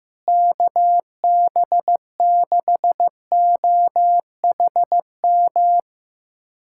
Morse code, heard as a keyed tone, about 15 wpm, 700 Hz. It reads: KB6OHM
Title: USA Callsign DE Callsign 15wpm